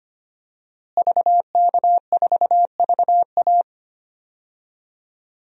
Morse code, heard as a keyed tone, about 25 wpm, 700 Hz. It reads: VK4VA
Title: DXCC-100 Call Signs - 5 Characters - Repeated 3x - 25wpm